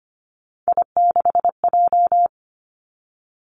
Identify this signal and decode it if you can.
Morse code, heard as a keyed tone, about 25 wpm, 700 Hz. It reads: I6J
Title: Three Challenging Letter-Number Combinations 25wpm